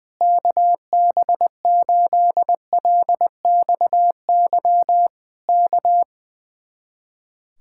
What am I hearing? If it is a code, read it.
Morse code, heard as a keyed tone, about 20 wpm, 700 Hz. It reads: KB8LXY K
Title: Speed-Racing Common QSO Phrases Encore - 20wpm